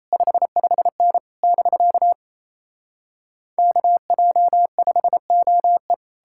Morse code, heard as a keyed tone, about 37 wpm, 700 Hz. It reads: 55N <BK> KJ5OE